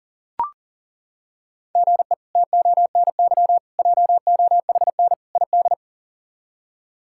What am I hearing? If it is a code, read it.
Morse code, heard as a keyed tone, about 40 wpm, 700 Hz. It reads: GE TONY JOHN ID
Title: K1USN SST QSO Practice - 40wpm